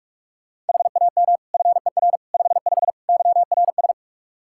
Morse code, heard as a keyed tone, about 44 wpm, 700 Hz. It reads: HAM FER 55 YRS
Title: K1USN SST QSO Practice 45wpm